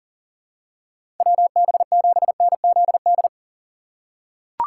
Morse code, heard as a keyed tone, about 37 wpm, 700 Hz. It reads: WB7NZD
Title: USA Call Signs - 6 Characters - 4x Character Spacing - 40wpm